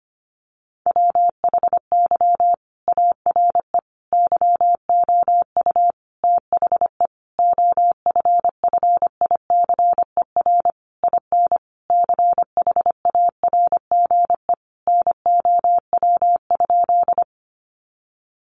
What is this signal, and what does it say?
Morse code, heard as a keyed tone, about 25 wpm, 700 Hz. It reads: WHY ARE YOU THE OFFICER IN CHARGE NOW?
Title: Sentences from Top 1200 Words 25wpm